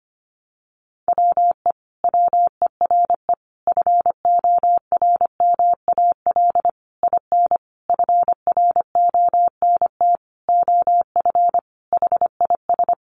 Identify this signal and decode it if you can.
Morse code, heard as a keyed tone, about 25 wpm, 700 Hz. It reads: WE WERE FORMAL IN FRONT OF HIS
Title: Sentences from Top 1300 Words 25wpm